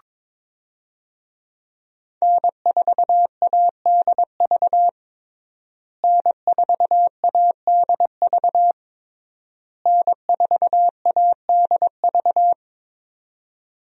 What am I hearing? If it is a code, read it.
Morse code, heard as a keyed tone, about 22 wpm, 700 Hz. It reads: N4ADV N4ADV N4ADV
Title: USA Call Signs - 5 Characters - Repeated 3x - 22wpm